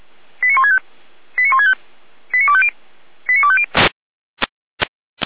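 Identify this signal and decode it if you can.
Five-tone selective call (ZVEI), heard as quick runs of five tones, tones of about 70 ms. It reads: 89165 89165 89250 89250
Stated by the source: ZVEI five-tone selective call received on SDR.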